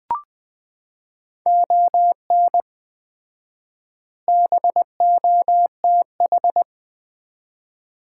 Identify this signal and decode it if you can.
Morse code, heard as a keyed tone, about 21 wpm, 700 Hz. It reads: ON BOTH
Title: 4x Word-Spacing - Sets of 4 Words - 20wpm